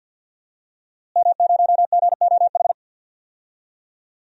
Morse code, heard as a keyed tone, about 50 wpm, 700 Hz. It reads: M0GOH
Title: DXCC-100 Call Signs - 5 Characters - Repeated 2x - 50wpm